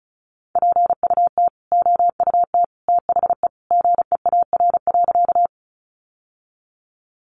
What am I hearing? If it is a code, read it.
Morse code, heard as a keyed tone, about 34 wpm, 700 Hz. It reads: PUT OUT THE GEAR.
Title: Sentences from Top 1400 Words 35wpm